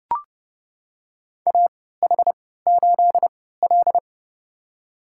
Morse code, heard as a keyed tone, about 30 wpm, spaced as for 17 wpm, 700 Hz. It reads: AH8L
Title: USA Call Signs - 4 Characters - 3x Character Spacing - 30wpm